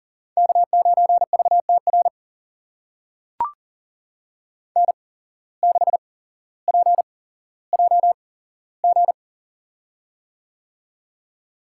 Morse code, heard as a keyed tone, about 42 wpm, spaced as for 17 wpm, 700 Hz. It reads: K9VTR N6PJG
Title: USA Call Signs - 5 Characters - 8x Character Spacing - 40wpm